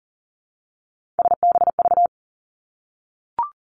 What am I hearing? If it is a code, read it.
Morse code, heard as a keyed tone, about 42 wpm, 700 Hz. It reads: SBV
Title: Three Challenging Letter-Number Combinations 40wpm